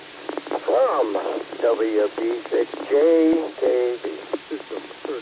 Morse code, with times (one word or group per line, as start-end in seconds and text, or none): none